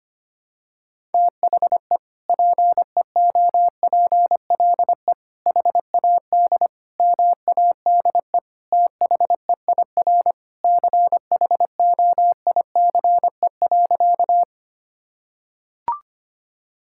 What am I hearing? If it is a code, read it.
Morse code, heard as a keyed tone, about 25 wpm, 700 Hz. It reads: THE PEOPLE HAD MADE THEIR CHOICE.
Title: Sentences from Top 1000 Words - Review Part 6 - 25wpm